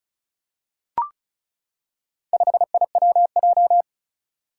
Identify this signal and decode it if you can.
Morse code, heard as a keyed tone, about 36 wpm, 700 Hz. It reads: 5IWJ
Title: Four Challenging Letter-Number Combinations 35wpm